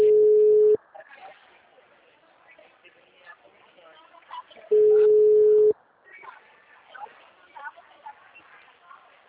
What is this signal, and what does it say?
Ringback tone (Europe, 425 Hz), with speech.